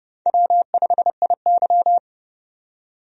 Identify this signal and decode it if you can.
Morse code, heard as a keyed tone, about 30 wpm, 700 Hz. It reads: W5IY
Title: USA Call Signs - 4 Characters - 30wpm